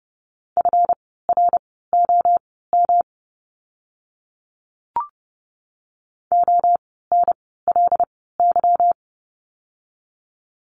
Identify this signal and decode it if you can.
Morse code, heard as a keyed tone, about 30 wpm, spaced as for 17 wpm, 700 Hz. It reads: FROM ONLY
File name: course / 30:10wpm / 230 Lesson 37f - Letter x - Words 30wpm